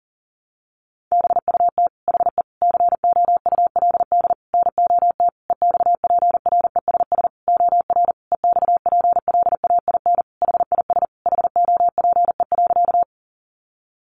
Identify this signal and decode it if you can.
Morse code, heard as a keyed tone, about 40 wpm, 700 Hz. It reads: BUT HE COULD NOT EXPRESS OR EXPLAIN HIS HOPE.
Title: Sentences from Top 1100 Words 40wpm